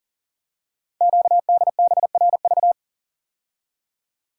Morse code, heard as a keyed tone, about 40 wpm, 700 Hz. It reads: QDBRV
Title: Code Groups - 5 Letters - 40wpm